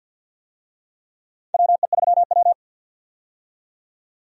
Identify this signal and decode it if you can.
Morse code, heard as a keyed tone, about 46 wpm, 700 Hz. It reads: WE3W
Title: USA Call Signs - 4 Characters - 50wpm